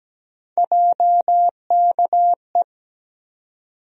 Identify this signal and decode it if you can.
Morse code, heard as a keyed tone, about 17 wpm, 700 Hz. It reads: JKE
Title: USA Call Sign Postfixes - 3 Letters 17wpm